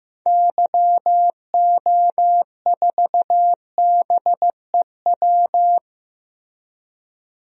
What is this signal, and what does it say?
Morse code, heard as a keyed tone, about 15 wpm, 700 Hz. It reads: YO4BEW
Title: DXCC-100 Call Signs - 6 Characters - Repeated 3x - 15wpm